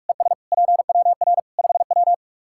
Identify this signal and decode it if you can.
Morse code, heard as a keyed tone, about 46 wpm, 700 Hz. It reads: ES PWR 5W
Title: Practice QSO - Exchange 2 - Info - 45wpm